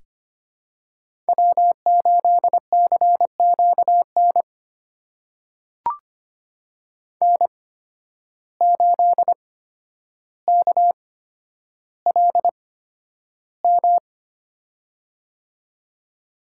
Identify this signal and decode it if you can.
Morse code, heard as a keyed tone, about 25 wpm, spaced as for 11 wpm, 700 Hz. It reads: W8CQN N8KLM
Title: USA Call Signs - 5 Characters - 8x Character Spacing - 25wpm